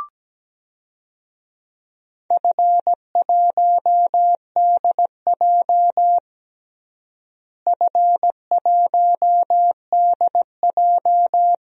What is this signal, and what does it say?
Morse code, heard as a keyed tone, about 17 wpm, 700 Hz. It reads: F1DJ F1DJ
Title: DXCC-100 Call Signs - 4 Characters - Repeated 3x - 17wpm